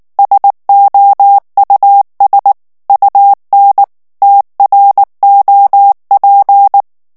Morse code, heard as a keyed tone, about 19 wpm, 800 Hz. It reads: SOUS UN TROP
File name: morse mp3 / FRANCAIS TEXTE CLAIR 1 VITESSE 18 MOTS MINUTES